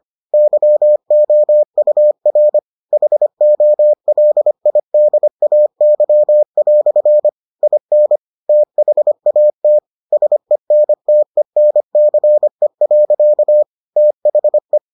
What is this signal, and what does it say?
Morse code, heard as a keyed tone, about 25 wpm, 600 Hz. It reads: YOUR HOLIDAY" IN THAT SENTENCE. THE